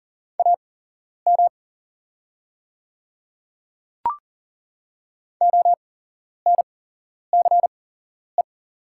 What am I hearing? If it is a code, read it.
Morse code, heard as a keyed tone, about 40 wpm, spaced as for 11 wpm, 700 Hz. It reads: AM ONCE
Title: 8x Character Spacing - Top 500 Words - 40wpm